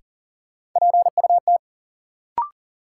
Morse code, heard as a keyed tone, about 39 wpm, 700 Hz. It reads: PUT
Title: Sentences from Top 200 Words - New Word Drill - 40wpm